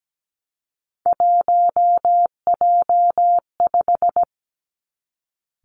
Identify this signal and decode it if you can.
Morse code, heard as a keyed tone, about 17 wpm, 700 Hz. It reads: 1J5
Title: Three Challenging Letter-Number Combinations 17wpm